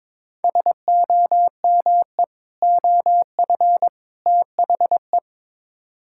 Morse code, heard as a keyed tone, about 22 wpm, 700 Hz. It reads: SOME OF THE
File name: practice / 22wpm / Build A Phrase - 5-Word Phrases - 22wpm